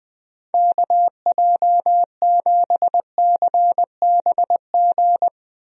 Morse code, heard as a keyed tone, about 20 wpm, 700 Hz. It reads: KJ7CBG
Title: USA Call Signs - 6 Characters - Repeated 2x - 20wpm